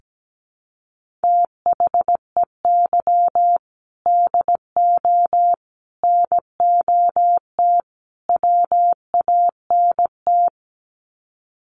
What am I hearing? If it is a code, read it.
Morse code, heard as a keyed tone, about 17 wpm, 700 Hz. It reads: THEY DO NOT WANT
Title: Sets of 4 Words 17wpm